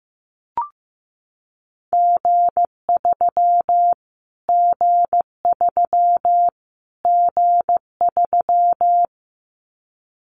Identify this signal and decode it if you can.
Morse code, heard as a keyed tone, about 15 wpm, 700 Hz. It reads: G3 G3 G3
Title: Callsign Prefixes for Top 25 DXCC - Repeated 3x - 15wpm